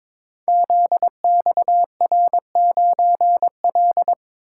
Morse code, heard as a keyed tone, about 22 wpm, 700 Hz. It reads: ZXR9L
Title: Code Groups - 5 Letter:Numbers - 22wpm